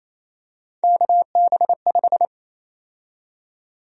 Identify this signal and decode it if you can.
Morse code, heard as a keyed tone, about 27 wpm, 700 Hz. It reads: KB5
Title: USA Call Sign Prefixes - 28wpm